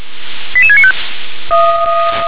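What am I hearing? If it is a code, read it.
Five-tone selective call (ZVEI), heard as quick runs of five tones, tones of about 70 ms. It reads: 88695
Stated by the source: ZVEI five-tone selective call received by SDR.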